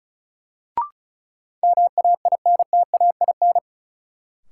Morse code, heard as a keyed tone, about 35 wpm, 700 Hz. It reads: MAINTAIN
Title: Sentences from Top 1000 Words - Review Part 5 - New Word Drill - 35wpm